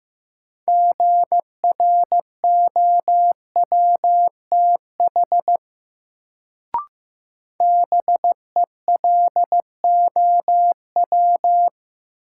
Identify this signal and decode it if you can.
Morse code, heard as a keyed tone, about 15 wpm, 700 Hz. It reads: GROWTH BELOW
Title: Sentences from Top 1000 Words - Review Part 5 - New Word Drill - 15wpm